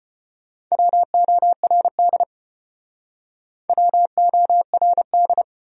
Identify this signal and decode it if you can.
Morse code, heard as a keyed tone, about 34 wpm, 700 Hz. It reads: WORD WORD